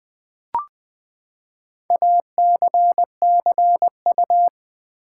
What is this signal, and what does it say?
Morse code, heard as a keyed tone, about 20 wpm, 700 Hz. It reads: ACCU
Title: Build A Word - 4 Letters At A Time - 8-Letter Words - 20wpm